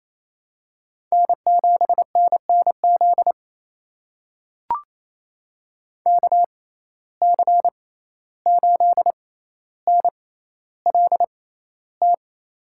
Morse code, heard as a keyed tone, about 28 wpm, spaced as for 14 wpm, 700 Hz. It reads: N7NNZ KC8NLT
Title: USA Call Signs - 6x Character Spacing - 28wpm